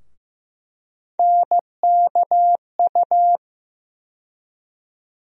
Morse code, heard as a keyed tone, about 15 wpm, 700 Hz. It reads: NKU
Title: USA Call Sign Postfixes - 3 Letters 15wpm